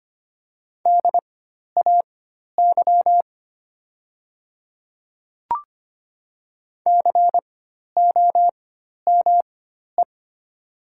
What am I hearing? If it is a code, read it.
Morse code, heard as a keyed tone, about 25 wpm, spaced as for 12 wpm, 700 Hz. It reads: DAY COME
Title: 4x Character Spacing - Top 100 Words - 25wpm